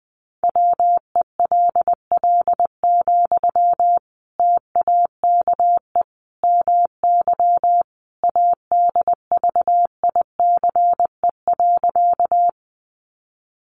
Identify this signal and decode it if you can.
Morse code, heard as a keyed tone, about 20 wpm, 700 Hz. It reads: WELL, TAKE MY ADVICE.
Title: Sentences from Top 1100 Words 20wpm